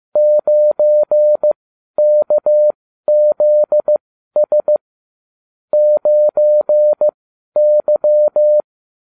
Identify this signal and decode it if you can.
Morse code, heard as a keyed tone, about 15 wpm, spaced as for 12 wpm, 600 Hz. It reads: 9KZS 9Y
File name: cw / ubacw30 3